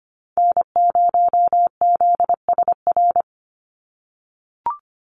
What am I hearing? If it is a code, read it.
Morse code, heard as a keyed tone, about 25 wpm, 700 Hz. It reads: N0ZSR
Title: USA Call Signs 25wpm